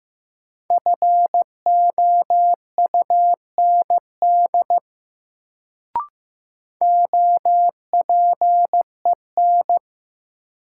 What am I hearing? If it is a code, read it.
Morse code, heard as a keyed tone, about 15 wpm, 700 Hz. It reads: FOUND OPEN
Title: Sentences from Top 300 Words - New Word Drill - 15wpm